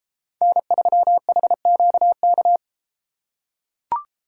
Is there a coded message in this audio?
Yes: Morse code from 0.4 to 2.6 s, heard as a keyed tone, about 31 wpm, 700 Hz. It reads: N3HQK